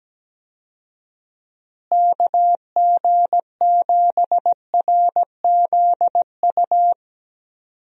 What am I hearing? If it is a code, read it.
Morse code, heard as a keyed tone, about 17 wpm, 700 Hz. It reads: KG7RZU
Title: USA Call Signs - 6 Characters - Repeated 3x - 17wpm